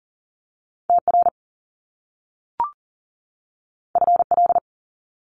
Morse code, heard as a keyed tone, about 41 wpm, 700 Hz. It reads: TR FL
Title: Top Two Letter Combinations in Words 40wpm